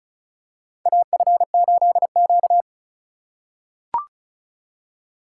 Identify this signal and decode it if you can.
Morse code, heard as a keyed tone, about 36 wpm, 700 Hz. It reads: AF8Q
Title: USA Call Signs - 4 Characters - 4x Character Spacing - 35wpm